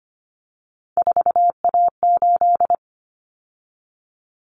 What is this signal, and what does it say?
Morse code, heard as a keyed tone, about 25 wpm, 700 Hz. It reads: 4A8
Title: Common Cut Numbers - 3-Digit 25wpm